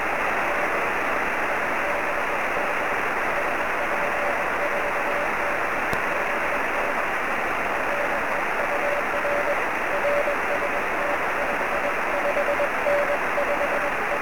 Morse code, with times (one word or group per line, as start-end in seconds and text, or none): none